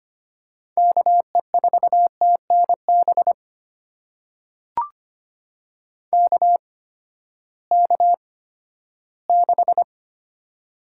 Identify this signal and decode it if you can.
Morse code, heard as a keyed tone, about 25 wpm, spaced as for 14 wpm, 700 Hz. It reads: KE4TNB KK6
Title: USA Call Signs - 6 Characters - 8x Character Spacing - 25wpm